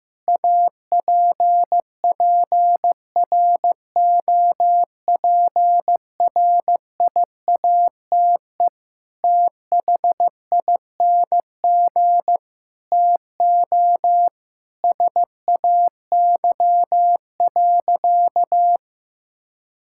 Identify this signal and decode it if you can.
Morse code, heard as a keyed tone, about 15 wpm, 700 Hz. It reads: APPROPRIATE THING TO SAY.
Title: Sentences from Top 1000 Words - Review Part 5 - 15wpm